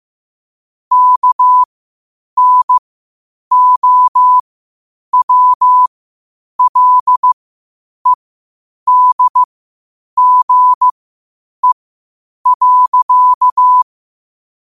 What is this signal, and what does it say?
Morse code, heard as a keyed tone, about 15 wpm, spaced as for 8 wpm, 1000 Hz. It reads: KNOWLEDGE.
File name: qotdfiles / 8wpm